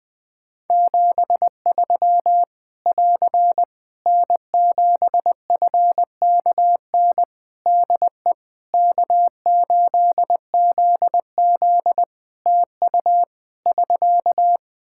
Morse code, heard as a keyed tone, about 20 wpm, 700 Hz. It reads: 73 <AR> N7FKN DE K8ZZ TU <SK>
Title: Practice QSO - Exchange 4 - Ending - 20wpm